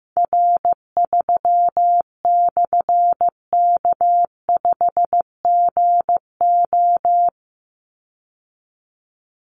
Morse code, heard as a keyed tone, about 15 wpm, 700 Hz. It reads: R3/K5GO
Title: Call Signs Repeated 3x - 15wpm